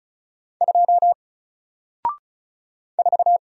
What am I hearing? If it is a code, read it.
Morse code, heard as a keyed tone, about 36 wpm, 700 Hz. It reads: 2 4
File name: Single-Letter-Number-35wpm